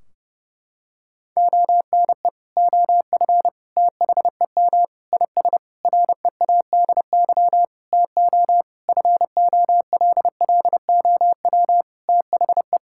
Morse code, heard as a keyed tone, about 30 wpm, 700 Hz. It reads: ONE OF THEM IS READY TO FOLLOW THE
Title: Sentences from Top 1600 Words 30wpm